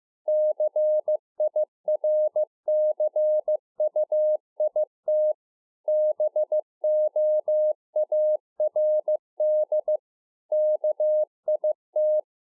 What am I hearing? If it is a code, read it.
Morse code, heard as a keyed tone, about 15 wpm, 610 Hz. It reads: CIRCUIT BOARD KIT
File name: 2011-04-mp3 / A SMALL TRANCEIVER 2 15WPM